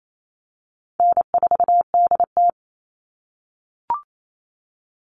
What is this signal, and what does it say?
Morse code, heard as a keyed tone, about 28 wpm, 700 Hz. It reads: N4DT
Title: Easy Call Signs - 28wpm